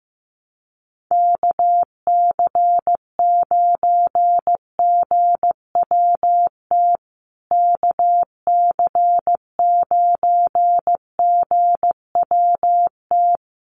Morse code, heard as a keyed tone, about 15 wpm, 700 Hz. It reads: KC9GWT KC9GWT
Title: USA Call Signs Repeated 3x 15wpm